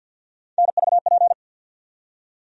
Morse code, heard as a keyed tone, about 51 wpm, 700 Hz. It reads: NVP
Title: USA Call Sign Postfixes - 3 Letters 50wpm